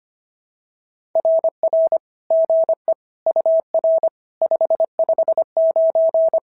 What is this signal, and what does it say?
Morse code, heard as a keyed tone, about 22 wpm, 650 Hz. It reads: RR GE UR 559
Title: SOTA QSO Practice 25wpm